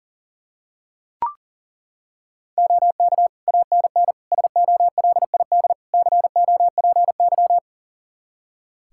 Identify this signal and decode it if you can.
Morse code, heard as a keyed tone, about 40 wpm, 700 Hz. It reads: OK ANN SOLID COPY
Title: K1USN SST QSO Practice 40wpm